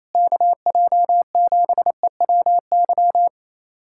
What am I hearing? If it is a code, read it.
Morse code, heard as a keyed tone, about 28 wpm, 700 Hz. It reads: KJ7EWY
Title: USA Callsign DE Callsign 28wpm